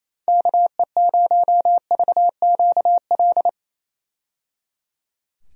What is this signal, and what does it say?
Morse code, heard as a keyed tone, about 28 wpm, 700 Hz. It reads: KE0VQL
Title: USA Call Signs - 6 Characters - 28wpm